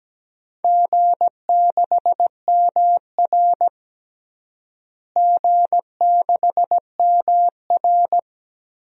Morse code, heard as a keyed tone, about 17 wpm, 700 Hz. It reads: G6MR G6MR
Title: DXCC-100 Call Signs - 4 Characters - Repeated 2x - 17wpm